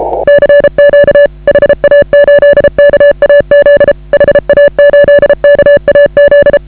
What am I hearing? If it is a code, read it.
Morse code, heard as a keyed tone, about 32 wpm, 560 Hz. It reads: CQ HA8KAZ HA8KAZ
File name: AUDIO2019 / HA8KAZ CW